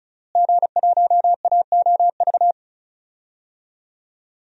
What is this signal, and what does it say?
Morse code, heard as a keyed tone, about 36 wpm, 700 Hz. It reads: G1AOV
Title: DXCC-100 Call Signs - 5 Characters - Repeated 3x - 35wpm